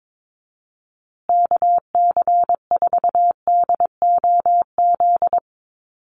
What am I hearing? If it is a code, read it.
Morse code, heard as a keyed tone, about 22 wpm, 690 Hz. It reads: KC4DOZ
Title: USA Call Signs Repeated 3x 22wpm